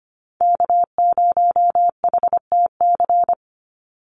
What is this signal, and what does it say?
Morse code, heard as a keyed tone, about 25 wpm, 700 Hz. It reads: K0HTC
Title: USA Call Signs 25wpm